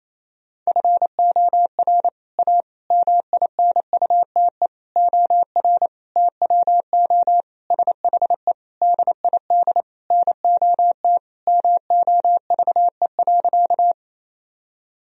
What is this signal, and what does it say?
Morse code, heard as a keyed tone, about 28 wpm, 700 Hz. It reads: FOR A MINUTE OR TWO SHE DID NOT MOVE.
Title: Sentences from Top 1000 Words - Review Part 1 - 28wpm